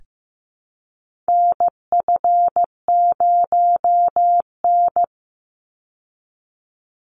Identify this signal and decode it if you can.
Morse code, heard as a keyed tone, about 15 wpm, 700 Hz. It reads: NF0N
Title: Call Signs Repeated 3x - 15wpm